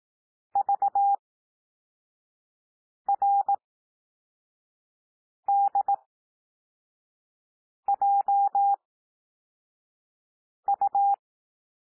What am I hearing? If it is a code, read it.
Morse code, heard as a keyed tone, about 18 wpm, spaced as for 4 wpm, 800 Hz. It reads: VRDJU
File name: cwmp3 / L2 SOUND LETTERS 18 wpm